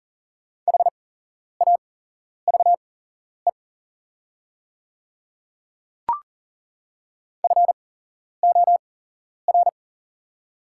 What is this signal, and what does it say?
Morse code, heard as a keyed tone, about 40 wpm, spaced as for 11 wpm, 700 Hz. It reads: HAVE FOR
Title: 8x Character Spacing - Top 100 Words - 40wpm